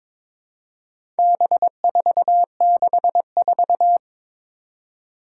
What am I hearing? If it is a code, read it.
Morse code, heard as a keyed tone, about 22 wpm, 700 Hz. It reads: B464
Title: Four Challenging Letter-Number Combinations 22wpm